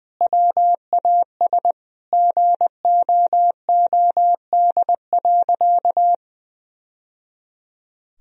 Morse code, heard as a keyed tone, about 20 wpm, 700 Hz. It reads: WAS GOOD.
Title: Sentences from Top 1000 Words - Review Part 4 - 20wpm